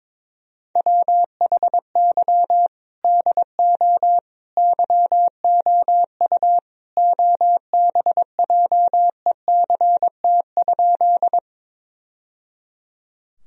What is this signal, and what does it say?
Morse code, heard as a keyed tone, about 22 wpm, 700 Hz. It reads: WHY DO YOU OBJECT?
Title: Sentences from Top 1000 Words - Review Part 3 - 22wpm